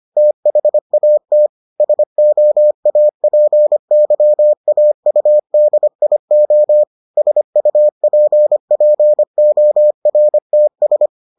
Morse code, heard as a keyed tone, about 25 wpm, 600 Hz. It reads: THAT SOAPYAUDIO SUPPORTS